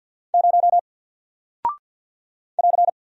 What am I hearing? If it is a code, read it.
Morse code, heard as a keyed tone, about 48 wpm, 700 Hz. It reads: 0 <AR>
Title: Single Character (Letter:Number:Punctuation:Prosign) 50wpm